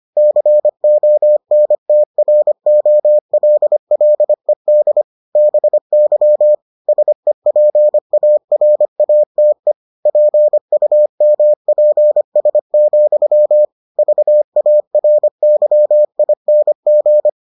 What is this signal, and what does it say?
Morse code, heard as a keyed tone, about 25 wpm, 600 Hz. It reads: CONTROLLED BY SEPARATE PUMPS, VARYING